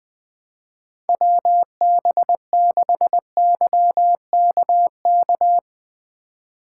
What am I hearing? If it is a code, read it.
Morse code, heard as a keyed tone, about 20 wpm, 700 Hz. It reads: WB6YKK
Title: USA Call Signs - 6 Characters - 8x Character Spacing - 20wpm